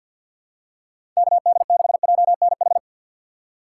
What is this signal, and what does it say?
Morse code, heard as a keyed tone, about 47 wpm, 700 Hz. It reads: KD6JNH
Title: USA Call Signs - 6 Characters - 4x Character Spacing - 50wpm